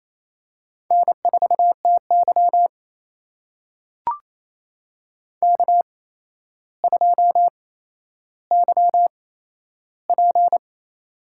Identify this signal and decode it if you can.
Morse code, heard as a keyed tone, about 28 wpm, spaced as for 12 wpm, 700 Hz. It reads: N4TY K2YP
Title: USA Call Signs - 4 Characters - 8x Character Spacing - 28wpm